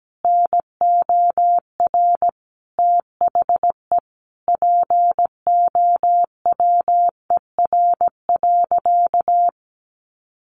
Morse code, heard as a keyed tone, about 17 wpm, 700 Hz. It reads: NOR THE POWER.
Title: Sentences from Top 1500 Words 17wpm